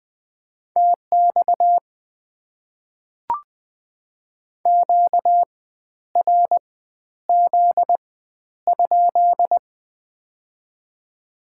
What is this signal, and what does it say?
Morse code, heard as a keyed tone, about 20 wpm, 690 Hz. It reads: TX Q R Z ?
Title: Top 90 QSO Elements - 4x Character Spacing - 20wpm